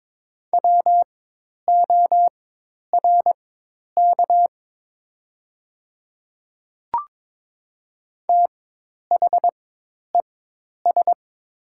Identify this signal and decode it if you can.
Morse code, heard as a keyed tone, about 22 wpm, spaced as for 10 wpm, 700 Hz. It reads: WORK THES
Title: 4x Character Spacing - Top 200 Words - 22wpm